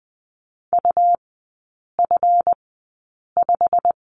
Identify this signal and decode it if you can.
Morse code, heard as a keyed tone, about 20 wpm, spaced as for 8 wpm, 700 Hz. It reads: UF5
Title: 184 Lesson 30e - Letter b - Three Characters 20wpm